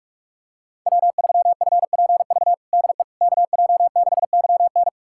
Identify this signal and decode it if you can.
Morse code, heard as a keyed tone, about 46 wpm, 700 Hz. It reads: W3FPV DE KJ6YN
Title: USA Callsign DE Callsign - 5 Characters - 45wpm